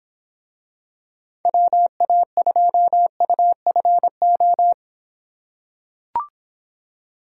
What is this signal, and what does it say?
Morse code, heard as a keyed tone, about 26 wpm, 700 Hz. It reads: WA2UFO